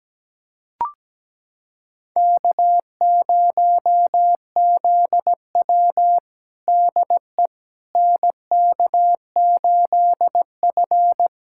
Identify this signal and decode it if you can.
Morse code, heard as a keyed tone, about 17 wpm, 700 Hz. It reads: K0ZW DE NK8F
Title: USA Callsign DE Callsign - 4 Characters - 17wpm